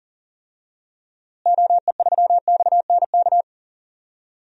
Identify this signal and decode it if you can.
Morse code, heard as a keyed tone, about 39 wpm, 700 Hz. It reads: OE3XNK
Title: DXCC-100 Call Signs - 6 Characters - Repeated 3x - 40wpm